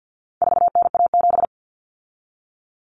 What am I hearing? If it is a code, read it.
Morse code, heard as a keyed tone, about 48 wpm, 700 Hz. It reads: 4NA7
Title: Common Cut Numbers - 4-Digit 50wpm